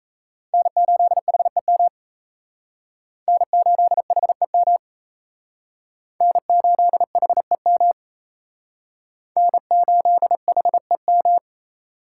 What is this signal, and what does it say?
Morse code, heard as a keyed tone, about 34 wpm, 700 Hz. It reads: N8HEM N8HEM N8HEM N8HEM
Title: Speed Racing - USA Call Signs 28wpm